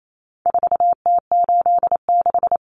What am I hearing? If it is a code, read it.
Morse code, heard as a keyed tone, about 28 wpm, 700 Hz. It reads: 4T86
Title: Common Cut Numbers - 4-Digit 28wpm